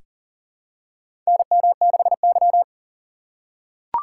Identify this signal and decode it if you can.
Morse code, heard as a keyed tone, about 40 wpm, 700 Hz. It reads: NM6Y
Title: USA Call Signs - 4 Characters - 2x Character Spacing - 40wpm